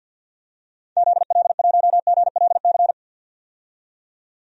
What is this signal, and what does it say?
Morse code, heard as a keyed tone, about 51 wpm, 700 Hz. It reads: ZL1GLC
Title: DXCC-100 Call Signs - 6 Characters - 50wpm